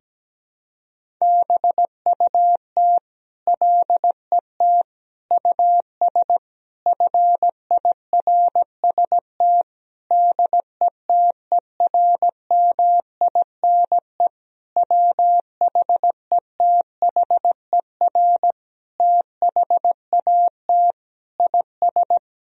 Morse code, heard as a keyed tone, about 17 wpm, 700 Hz. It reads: BUT LET US FIRST DETERMINE WHETHER THAT IS
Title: Sentences from Top 1000 Words - Review Part 4 - 17wpm